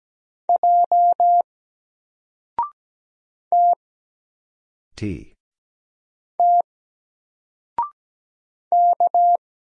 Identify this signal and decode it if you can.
Morse code, heard as a keyed tone, about 17 wpm, 700 Hz. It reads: J T T K
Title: Single Letter-Number 17wpm